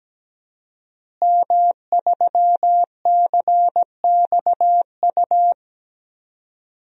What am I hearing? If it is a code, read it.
Morse code, heard as a keyed tone, about 17 wpm, 700 Hz. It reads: M3CXU
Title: DXCC-100 Call Signs - 5 Characters - Repeated 3x - 17wpm